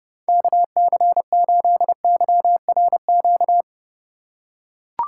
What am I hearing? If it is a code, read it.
Morse code, heard as a keyed tone, about 32 wpm, 700 Hz. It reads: KC8YRQ